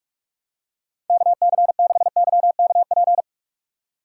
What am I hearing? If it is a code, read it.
Morse code, heard as a keyed tone, about 46 wpm, 700 Hz. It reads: KC6YKP
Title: USA Call Signs - 3x Character Spacing - 45wpm